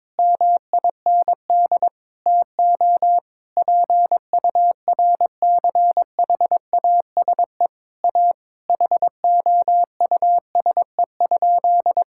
Morse code, heard as a keyed tone, about 22 wpm, 700 Hz. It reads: MIND TO PURCHASE A HOUSE?
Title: Sentences from Top 1000 Words - Review Part 6 - 22wpm